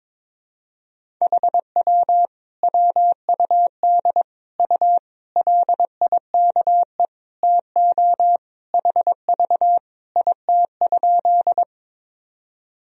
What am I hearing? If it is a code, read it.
Morse code, heard as a keyed tone, about 22 wpm, 700 Hz. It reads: HW WUD U LIKE TO HV IT?
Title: Sentences from Top 100 Words and Abbreviations 22wpm